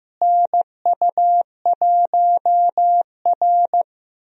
Morse code, heard as a keyed tone, about 15 wpm, 700 Hz. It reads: NU1R
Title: USA Call Signs - 4 Characters - Repeated 2x - 15wpm